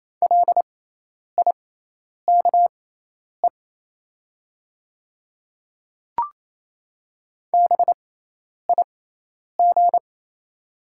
Morse code, heard as a keyed tone, about 28 wpm, spaced as for 10 wpm, 700 Hz. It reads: LIKE BIG
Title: 6x Character Spacing - Top 300 Words - 28wpm